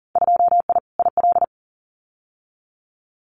Morse code, heard as a keyed tone, about 40 wpm, 700 Hz. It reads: 2I IL
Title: Winter Field Day Exchanges Encore 40wpm